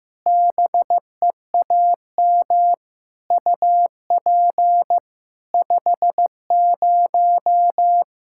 Morse code, heard as a keyed tone, about 15 wpm, 700 Hz. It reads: BEAM UP 50
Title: K1USN SST QSO Practice 15wpm